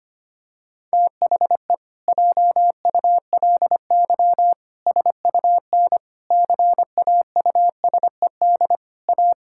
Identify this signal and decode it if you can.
Morse code, heard as a keyed tone, about 25 wpm, 700 Hz. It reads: THE JULY SUN CAUSED A
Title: Pangram Challenge v2 - 25wpm